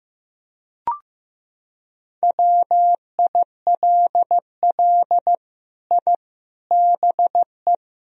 Morse code, heard as a keyed tone, about 15 wpm, 700 Hz. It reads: WILL I BE
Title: Sentences from Top 1000 Words - Review Part 3 - 15wpm